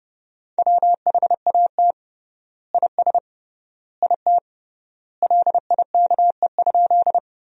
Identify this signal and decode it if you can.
Morse code, heard as a keyed tone, about 32 wpm, 700 Hz. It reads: WHAT IS IT LIKE?
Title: 3x Word-Spacing - Sentences from Top 100 Words - 30wpm